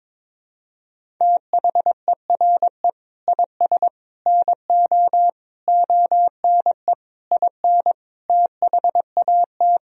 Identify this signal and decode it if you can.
Morse code, heard as a keyed tone, about 22 wpm, 700 Hz. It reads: THERE IS NO ONE IN THAT
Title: Sentences from Top 500 Words and Abbreviations 22wpm